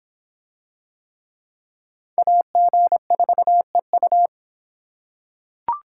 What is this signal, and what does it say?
Morse code, heard as a keyed tone, about 26 wpm, 700 Hz. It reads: AG4EU